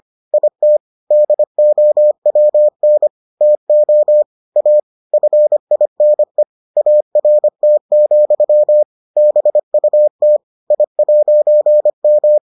Morse code, heard as a keyed tone, about 25 wpm, 600 Hz. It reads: IT DOWN TO A FINE ART, BUT I'M